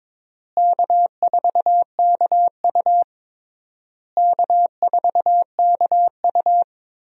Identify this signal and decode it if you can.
Morse code, heard as a keyed tone, about 22 wpm, 700 Hz. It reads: K4KU K4KU
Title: USA Call Signs - 4 Characters - Repeated 3x - 22wpm